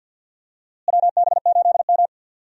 Morse code, heard as a keyed tone, about 46 wpm, 700 Hz. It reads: WB8M
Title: USA Call Signs - 4 Characters - Repeated 3x - 50wpm